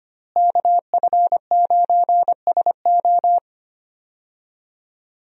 Morse code, heard as a keyed tone, about 25 wpm, 700 Hz. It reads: KF9SO
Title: USA Call Signs - 5 Characters - Repeated 3x - 25wpm